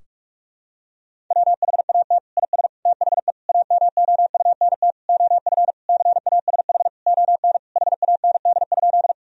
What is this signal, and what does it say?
Morse code, heard as a keyed tone, about 46 wpm, 700 Hz. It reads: WHAT IS THE AMOUNT OF CASH ON HAND?
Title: Sentences from Top 1000 Words - Review Part 4 - 45wpm